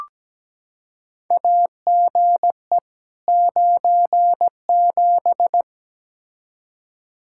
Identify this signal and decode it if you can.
Morse code, heard as a keyed tone, about 17 wpm, 700 Hz. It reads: AGE 97
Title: Practice QSO - Exchange 3 - More Info - 17wpm